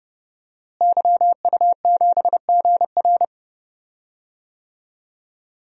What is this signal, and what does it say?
Morse code, heard as a keyed tone, about 30 wpm, 700 Hz. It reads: YU7GR
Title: DXCC-100 Call Signs - 5 Characters - Repeated 2x - 30wpm